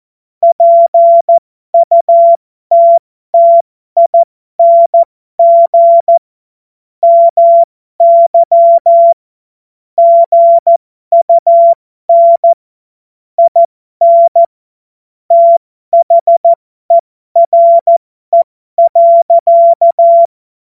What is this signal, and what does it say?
Morse code, heard as a keyed tone, about 14 wpm, spaced as for 12 wpm, 680 Hz. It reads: PUTTING MY GUN IN THERE.